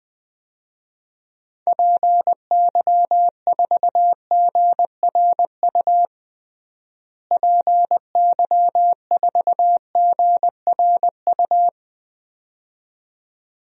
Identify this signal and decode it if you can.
Morse code, heard as a keyed tone, about 20 wpm, 700 Hz. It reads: PY4GRU PY4GRU
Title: DXCC-100 Call Signs - 6 Characters - Repeated 2x - 20wpm